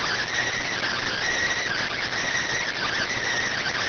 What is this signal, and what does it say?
Emergency Alert System (SAME) tones, heard as harsh buzzing bursts. It is heard from 0.6 to 3.2 s.